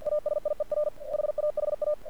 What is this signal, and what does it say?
Morse code, heard as a keyed tone, about 47 wpm, 600 Hz. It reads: ASIER HAHA
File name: archived / CRSnet